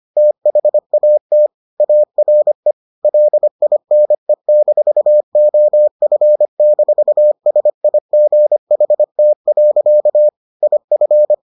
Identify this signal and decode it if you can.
Morse code, heard as a keyed tone, about 25 wpm, 600 Hz. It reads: THAT ARE LINE-OF-SIGHT. IF